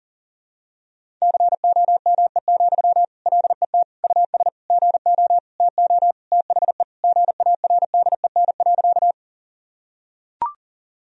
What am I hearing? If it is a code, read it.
Morse code, heard as a keyed tone, about 40 wpm, 700 Hz. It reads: COME, LET US GO TO THE GARDEN.
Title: Sentences from Top 1000 Words - Review Part 4 - 40wpm